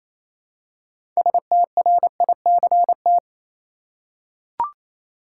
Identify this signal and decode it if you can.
Morse code, heard as a keyed tone, about 28 wpm, 700 Hz. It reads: STRICT
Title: Six Letter Words - 28wpm